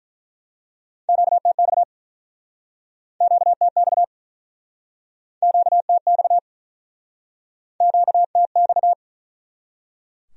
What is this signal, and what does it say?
Morse code, heard as a keyed tone, about 49 wpm, 700 Hz. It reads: QTX QTX QTX QTX